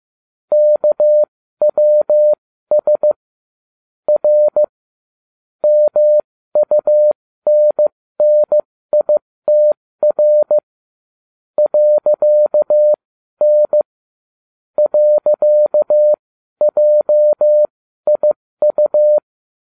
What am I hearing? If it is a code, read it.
Morse code, heard as a keyed tone, about 15 wpm, spaced as for 12 wpm, 600 Hz. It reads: KWS R MUNNITR .N .JIU